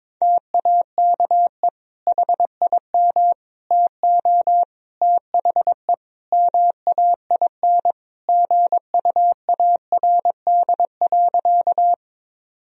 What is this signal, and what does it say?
Morse code, heard as a keyed tone, about 22 wpm, 700 Hz. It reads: TAKE HIM TO THE MAIN GUARD.
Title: Sentences from Top 1000 Words - Review Part 4 - 22wpm